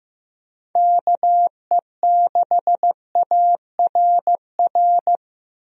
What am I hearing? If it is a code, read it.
Morse code, heard as a keyed tone, about 15 wpm, 700 Hz. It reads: KE6ARR
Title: USA Call Signs - 6 Characters - 15wpm